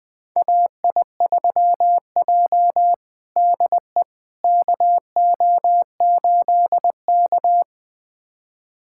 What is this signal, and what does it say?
Morse code, heard as a keyed tone, about 21 wpm, 700 Hz. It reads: AI3J DE KO8K
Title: USA Callsign DE Callsign - 4 Characters - 20wpm